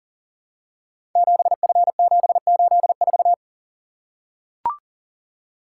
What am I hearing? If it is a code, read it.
Morse code, heard as a keyed tone, about 38 wpm, 700 Hz. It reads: 7F784
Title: Code Groups - 5 Letter:Numbers - 40wpm